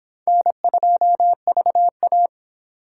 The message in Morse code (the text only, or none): N2VA